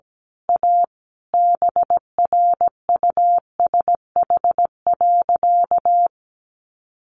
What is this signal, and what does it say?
Morse code, heard as a keyed tone, about 17 wpm, 700 Hz. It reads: A BRUSH.
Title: Sentences from Top 1100 Words 17wpm